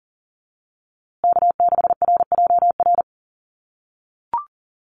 Morse code, heard as a keyed tone, about 40 wpm, 700 Hz. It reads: K6RJR
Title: USA Call Signs Repeated 2x 40wpm